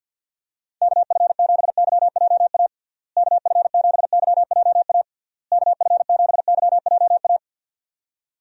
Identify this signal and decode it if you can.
Morse code, heard as a keyed tone, about 51 wpm, 700 Hz. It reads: KF7YJA KF7YJA KF7YJA
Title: USA Call Signs - 6 Characters - Repeated 3x - 50wpm